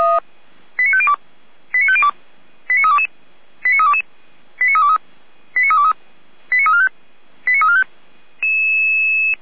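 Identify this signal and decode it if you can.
Five-tone selective call (ZVEI), heard as quick runs of five tones, tones of about 80 ms. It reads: 89502 89502 89322 89322 89323 89323 89345 89345
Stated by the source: ZVEI five-tone selective call received by SDR, from the feuerwehr-orlamuende.org fire brigade archive.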